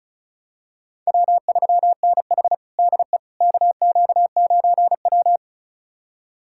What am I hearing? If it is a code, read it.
Morse code, heard as a keyed tone, about 34 wpm, 700 Hz. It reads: W3NH DE KQ9W
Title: USA Callsign DE Callsign - 4 Characters - 35wpm